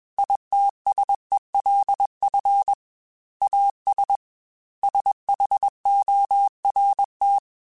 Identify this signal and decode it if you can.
Morse code, heard as a keyed tone, about 21 wpm, 800 Hz. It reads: ITSELF AS SHORT
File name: Technical Subjects / Technical Subjects-mp3s / XC @20